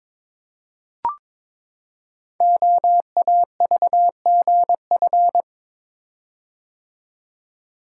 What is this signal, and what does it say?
Morse code, heard as a keyed tone, about 22 wpm, 700 Hz. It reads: OAVGF
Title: Code Groups - 5 Letters - 22wpm